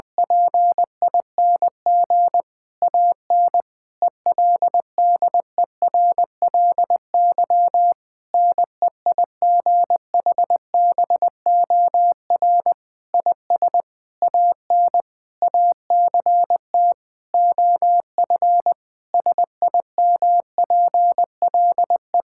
Morse code, heard as a keyed tone, about 20 wpm, 700 Hz. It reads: PING AN ELDERLY NEIGHBOR IS AN ACT OF SIMPLE
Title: Difficult Sentences with Hints - 20wpm